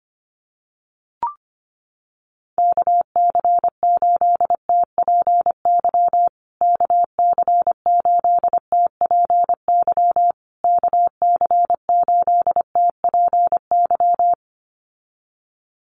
Morse code, heard as a keyed tone, about 25 wpm, 700 Hz. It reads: KC8TPY KC8TPY KC8TPY
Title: USA Call Signs Repeated 3x 25wpm